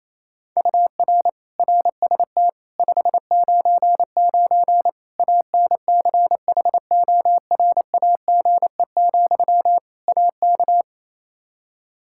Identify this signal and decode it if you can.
Morse code, heard as a keyed tone, about 28 wpm, 700 Hz. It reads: UR RST 599 ANCHORAGE, AK
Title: K1USN SST QSO Practice 28wpm